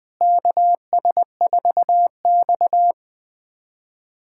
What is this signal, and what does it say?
Morse code, heard as a keyed tone, about 21 wpm, 700 Hz. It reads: KS4X
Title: USA Callsign DE Callsign - 4 Characters - 20wpm